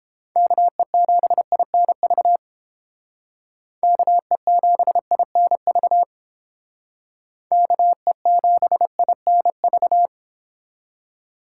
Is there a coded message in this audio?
Yes: Morse code from 0.4 to 10.1 s, heard as a keyed tone, about 28 wpm, 700 Hz. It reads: KE7INV KE7INV KE7INV